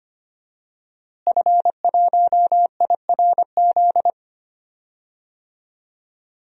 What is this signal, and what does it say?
Morse code, heard as a keyed tone, about 25 wpm, 700 Hz. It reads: F1IRZ
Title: DXCC-100 Call Signs - 5 Characters - Repeated 3x - 25wpm